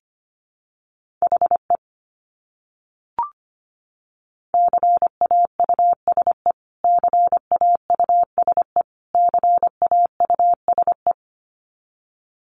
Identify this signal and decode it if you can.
Morse code, heard as a keyed tone, about 25 wpm, 700 Hz. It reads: HE CAUSE CAUSE CAUSE
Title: Top 200 Words Repeated 25wpm